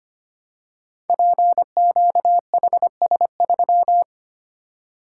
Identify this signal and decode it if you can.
Morse code, heard as a keyed tone, about 25 wpm, 700 Hz. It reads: PQHS3
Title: Code Groups - 5 Letter:Numbers - 25wpm